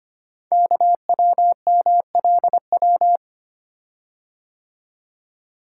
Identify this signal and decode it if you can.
Morse code, heard as a keyed tone, about 25 wpm, 700 Hz. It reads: KWMLW
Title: Code Groups - 5 Letters - 25wpm